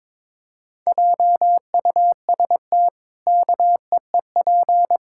Morse code, heard as a keyed tone, about 21 wpm, 700 Hz. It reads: JUST KEEP
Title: Pangram Challenge v2 - 22wpm